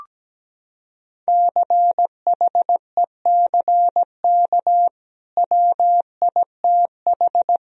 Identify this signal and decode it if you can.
Morse code, heard as a keyed tone, about 17 wpm, 700 Hz. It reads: CHECK WITH
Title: Sentences from Top 1000 Words - Review Part 3 - 17wpm